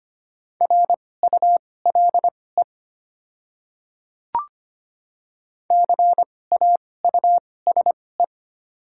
Morse code, heard as a keyed tone, about 25 wpm, spaced as for 18 wpm, 700 Hz. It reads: RULE CAUSE
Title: 2x Character Spacing - Top 500 Words - 25wpm